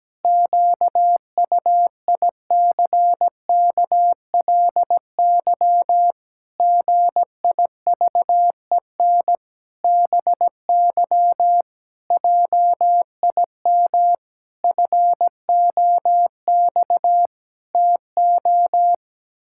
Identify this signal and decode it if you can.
Morse code, heard as a keyed tone, about 17 wpm, 700 Hz. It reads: QUICKLY GIVEN BY JIM FOX TO
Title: Pangram Challenge v2 - 17wpm